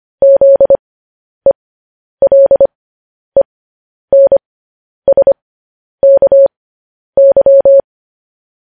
Morse code, heard as a keyed tone, about 25 wpm, spaced as for 10 wpm, 550 Hz. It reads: ZELENSKY
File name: podcast / News headlines-1x-25x10